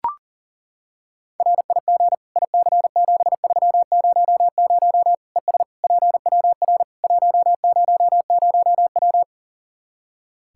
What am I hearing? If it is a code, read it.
Morse code, heard as a keyed tone, about 40 wpm, 700 Hz. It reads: RIG IC7300 ES PWR 100W
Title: Practice QSO - Exchange 2 - Info - 40wpm